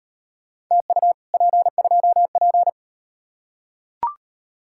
Morse code, heard as a keyed tone, about 38 wpm, 700 Hz. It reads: TU P2P